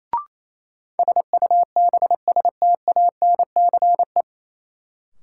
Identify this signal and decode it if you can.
Morse code, heard as a keyed tone, about 27 wpm, 700 Hz. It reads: SUBSTANCE
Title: Sentences from Top 1500 Words - New Word Drill Alt - 28wpm